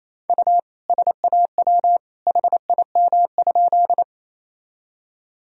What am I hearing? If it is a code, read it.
Morse code, heard as a keyed tone, about 28 wpm, 700 Hz. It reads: U SAW HIM?
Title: Sentences from Top 300 Words and Abbreviations 28wpm